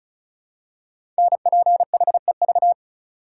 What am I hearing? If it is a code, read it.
Morse code, heard as a keyed tone, about 36 wpm, 690 Hz. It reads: NPHEV
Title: Code Groups - 5 Letters - 35wpm